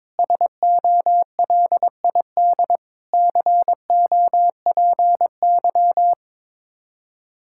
Morse code, heard as a keyed tone, about 22 wpm, 700 Hz. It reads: SOLID COPY
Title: Practice QSO - Exchange 3 - More Info - 22wpm